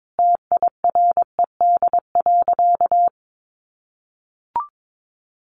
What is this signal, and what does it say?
Morse code, heard as a keyed tone, about 22 wpm, 700 Hz. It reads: TIRED.
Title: Easy Phrases 22wpm